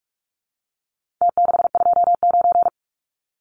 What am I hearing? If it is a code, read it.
Morse code, heard as a keyed tone, about 44 wpm, 700 Hz. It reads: T629
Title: Numbers - Contest Style - 4-Digit 45wpm